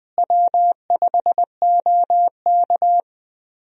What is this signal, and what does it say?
Morse code, heard as a keyed tone, about 20 wpm, 700 Hz. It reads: W5OK
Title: USA Call Signs - 4 Characters - Repeated 2x - 20wpm